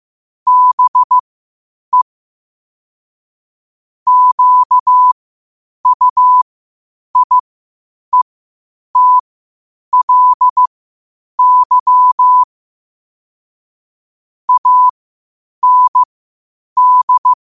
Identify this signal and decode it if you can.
Morse code, heard as a keyed tone, about 15 wpm, spaced as for 8 wpm, 1000 Hz. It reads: BE QUIETLY AND
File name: qotdfiles / 8wpm